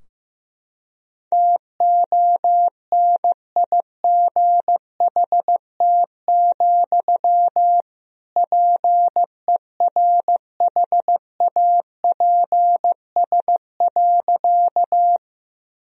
Morse code, heard as a keyed tone, about 15 wpm, 700 Hz. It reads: TONIGHT, PERHAPS.
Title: Sentences from Top 1000 Words - Review Part 3 - 15wpm